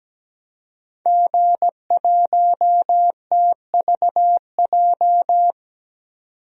Morse code, heard as a keyed tone, about 17 wpm, 700 Hz. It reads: G1TVJ
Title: DXCC-100 Call Signs - 5 Characters - 17wpm